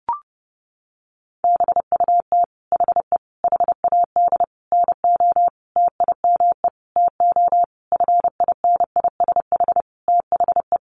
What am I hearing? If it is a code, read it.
Morse code, heard as a keyed tone, about 30 wpm, 700 Hz. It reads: BUT HE HAD NO TIME TO FINISH THE
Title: Sentences from Top 1200 Words 30wpm